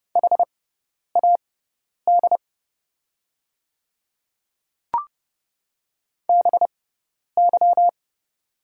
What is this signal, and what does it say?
Morse code, heard as a keyed tone, about 30 wpm, spaced as for 10 wpm, 700 Hz. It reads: HAD BY
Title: 6x Character Spacing - Top 200 Words - 30wpm